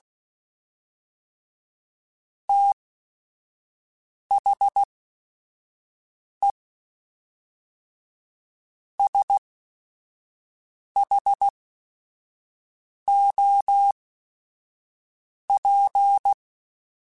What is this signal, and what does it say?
Morse code, heard as a keyed tone, about 16 wpm, spaced as for 5 wpm, 800 Hz. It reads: THE SHOP
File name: Weather Sets / Weather Sets-mp3s / Weather 4 28 @5